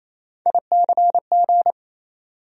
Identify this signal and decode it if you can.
Morse code, heard as a keyed tone, about 28 wpm, 700 Hz. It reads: ICG
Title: USA Call Sign Postfixes - 3 Letters 28wpm